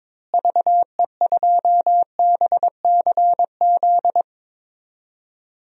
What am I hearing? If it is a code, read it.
Morse code, heard as a keyed tone, about 21 wpm, 700 Hz. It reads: VE2BCZ
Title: DXCC-100 Call Signs - 6 Characters - Repeated 2x - 22wpm